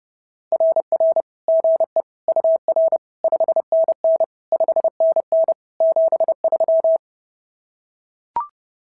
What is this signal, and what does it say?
Morse code, heard as a keyed tone, about 30 wpm, 590 Hz. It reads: RR GE UR 5NN 5NN 73
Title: SOTA QSO Practice 30wpm